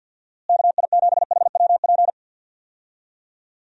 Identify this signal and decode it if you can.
Morse code, heard as a keyed tone, about 50 wpm, 700 Hz. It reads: KI7HWP
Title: USA Call Signs - 6 Characters - Repeated 3x - 50wpm